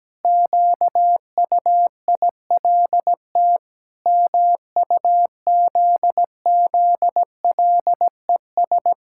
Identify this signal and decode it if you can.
Morse code, heard as a keyed tone, about 17 wpm, 700 Hz. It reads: QUILT MUZZLES
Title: Pangram Challenge v2 - 17wpm